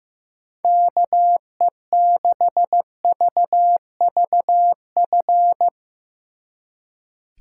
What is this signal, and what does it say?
Morse code, heard as a keyed tone, about 16 wpm, 700 Hz. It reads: KE6VVF
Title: Speed Racing - USA 2x3 Call Signs 15wpm